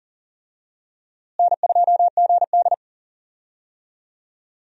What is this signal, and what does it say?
Morse code, heard as a keyed tone, about 40 wpm, 700 Hz. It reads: N2GD
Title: USA Call Signs - 4 Characters - Repeated 2x - 40wpm